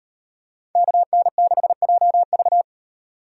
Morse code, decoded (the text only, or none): KN6JV